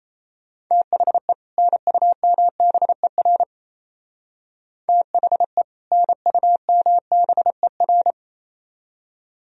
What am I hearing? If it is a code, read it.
Morse code, heard as a keyed tone, about 36 wpm, 700 Hz. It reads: THE NUMBER THE NUMBER